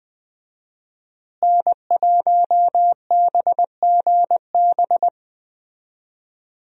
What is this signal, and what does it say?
Morse code, heard as a keyed tone, about 20 wpm, 700 Hz. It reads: N1BGB
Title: USA Call Signs - 5 Characters - Repeated 2x - 20wpm